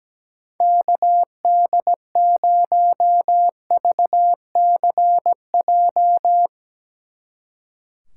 Morse code, heard as a keyed tone, about 18 wpm, 700 Hz. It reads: KD0VCJ
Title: Speed Racing - USA 2x3 Call Signs 17wpm